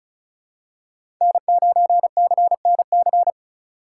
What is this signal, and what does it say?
Morse code, heard as a keyed tone, about 36 wpm, 700 Hz. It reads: N9CNC
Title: USA Call Signs - 5 Characters - 4x Character Spacing - 35wpm